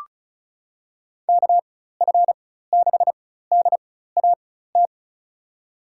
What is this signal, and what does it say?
Morse code, heard as a keyed tone, about 36 wpm, spaced as for 16 wpm, 700 Hz. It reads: KF6DAT
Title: USA Call Signs - 6 Characters - 4x Character Spacing - 35wpm